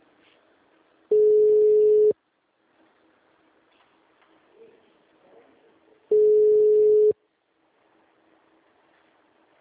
Ringback tone (Europe, 425 Hz).
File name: out- -2010- - - .1656